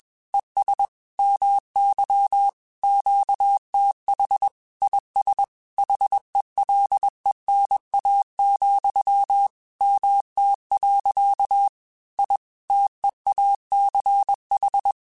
Morse code, heard as a keyed tone, about 21 wpm, 800 Hz. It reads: ES MY QTH IS HELENA, MT. I TEACH
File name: QSOs / QSOs-mp3s / WD0BGP 1@20